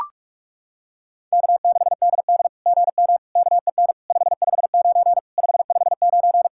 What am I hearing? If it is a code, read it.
Morse code, heard as a keyed tone, about 44 wpm, 660 Hz. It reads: K6DD GM KEN 559 559
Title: SOTA QSO Practice 45wpm